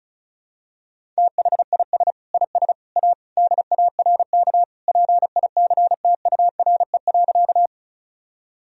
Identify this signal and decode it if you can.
Morse code, heard as a keyed tone, about 35 wpm, 700 Hz. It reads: THIS IS A DARK PICTURE.
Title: Sentences from Top 1000 Words - Review Part 1 - 35wpm